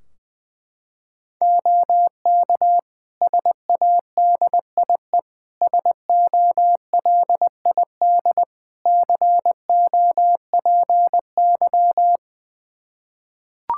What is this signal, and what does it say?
Morse code, heard as a keyed tone, about 21 wpm, 700 Hz. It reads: OK SADIE SOLID COPY
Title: Practice QSO - Exchange 3 - More Info - 20wpm